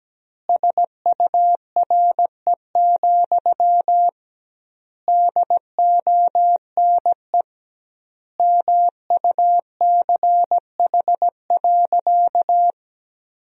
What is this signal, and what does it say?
Morse code, heard as a keyed tone, about 17 wpm, 700 Hz. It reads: SURE, DONE MUCH.
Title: 2x Word-Spacing - Sentences from Top 500 Words - 17wpm